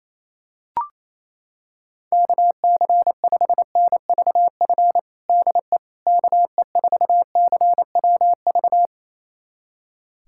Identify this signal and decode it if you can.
Morse code, heard as a keyed tone, about 28 wpm, 700 Hz. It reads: KC5NVF DE KE4CWV
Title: USA Callsign DE Callsign 28wpm